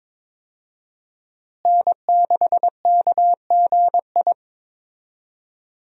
Morse code, heard as a keyed tone, about 22 wpm, 700 Hz. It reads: N6KGI
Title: USA Call Signs - 5 Characters - Repeated 2x - 22wpm